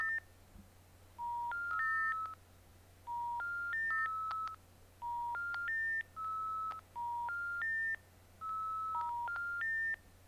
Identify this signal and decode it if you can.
Special information tone, with noise.
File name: pdsounds / dial-up connection failed